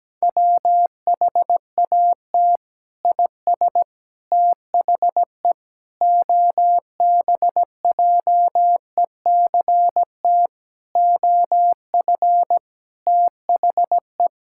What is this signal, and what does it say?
Morse code, heard as a keyed tone, about 17 wpm, 700 Hz. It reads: WHAT IS THE OBJECT OF THE
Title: Sentences from Top 1000 Words - Review Part 5 - 17wpm